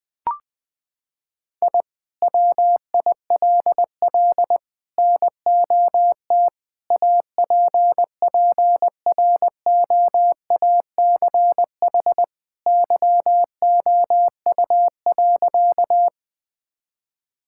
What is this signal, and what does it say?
Morse code, heard as a keyed tone, about 20 wpm, 700 Hz. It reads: I WILL NOT APPROACH YOU.
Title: Sentences from Top 1000 Words - Review Part 6 - 20wpm